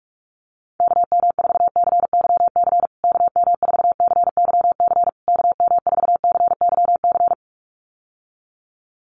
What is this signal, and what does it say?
Morse code, heard as a keyed tone, about 44 wpm, 700 Hz. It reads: KM4CYC KM4CYC KM4CYC
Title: USA Call Signs Repeated 3x 45wpm